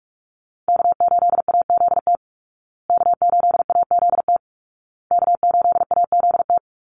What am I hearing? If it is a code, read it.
Morse code, heard as a keyed tone, about 44 wpm, 700 Hz. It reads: K8AZT K8AZT K8AZT
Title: Call Signs Repeated 2x - 45wpm